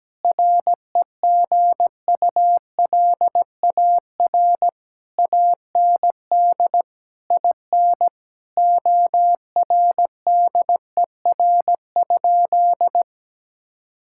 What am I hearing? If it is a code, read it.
Morse code, heard as a keyed tone, about 17 wpm, 700 Hz. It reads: REGULAR AND IN ORDER?
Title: Sentences from Top 1000 Words - Review Part 6 - 17wpm